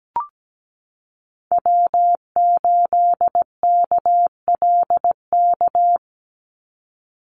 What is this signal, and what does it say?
Morse code, heard as a keyed tone, about 17 wpm, 700 Hz. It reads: W8KLK
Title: USA Call Signs 17wpm